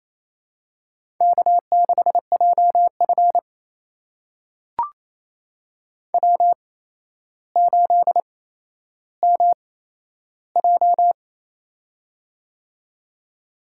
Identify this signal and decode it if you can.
Morse code, heard as a keyed tone, about 28 wpm, spaced as for 12 wpm, 700 Hz. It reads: K6JF W8MJ
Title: USA Call Signs - 4 Characters - 8x Character Spacing - 28wpm